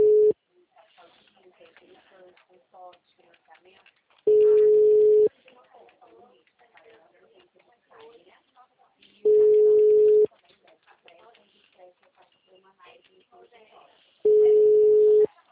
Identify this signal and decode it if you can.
Ringback tone (Europe, 425 Hz), with speech.
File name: out- -2012- - - .20940